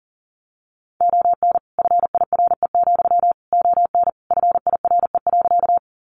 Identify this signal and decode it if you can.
Morse code, heard as a keyed tone, about 40 wpm, 700 Hz. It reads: ON FIRE, ON FIRE.
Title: Sentences from Top 400 Words 40wpm